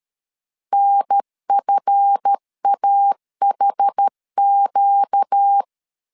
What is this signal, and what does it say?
Morse code, heard as a keyed tone, about 13 wpm, 800 Hz. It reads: NFAHQ
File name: LEZIONI / 7 GF9Z / 7 GF9 12wpm 2319